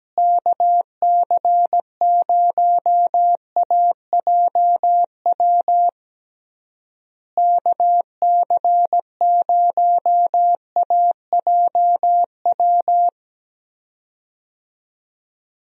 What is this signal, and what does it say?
Morse code, heard as a keyed tone, about 17 wpm, 700 Hz. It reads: KC0AJW KC0AJW
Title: USA Call Signs - 6 Characters - Repeated 3x - 17wpm